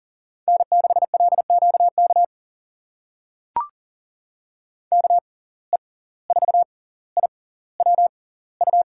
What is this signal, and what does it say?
Morse code, heard as a keyed tone, about 40 wpm, spaced as for 19 wpm, 700 Hz. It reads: N6LQK KE4IWU
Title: USA Call Signs - 6x Character Spacing - 40wpm